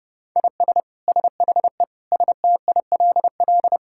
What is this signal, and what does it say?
Morse code, heard as a keyed tone, about 32 wpm, 700 Hz. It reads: IS SHE STILL
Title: Sentences from Top 1000 Words - Review Part 6 - 30wpm